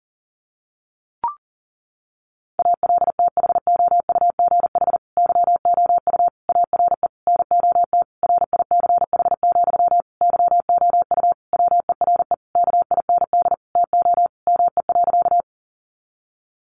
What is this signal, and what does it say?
Morse code, heard as a keyed tone, about 40 wpm, 700 Hz. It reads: ALTHOUGH YOU ARE NOT RICH, YOU WERE KIND TO ME.
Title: Sentences from Top 1100 Words 40wpm